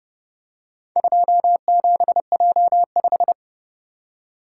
Morse code, heard as a keyed tone, about 29 wpm, 700 Hz. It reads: 27J5